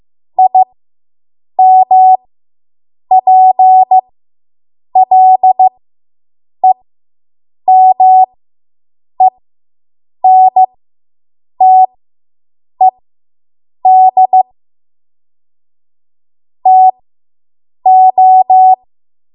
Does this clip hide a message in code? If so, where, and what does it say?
Yes: Morse code from 0.4 to 18.7 s, heard as a keyed tone, about 15 wpm, spaced as for 7 wpm, 750 Hz. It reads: IMPLEMENTED TO